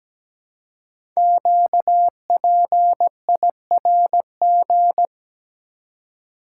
Morse code, heard as a keyed tone, about 17 wpm, 700 Hz. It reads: QPIRG
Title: Code Groups - 5 Letters - 17wpm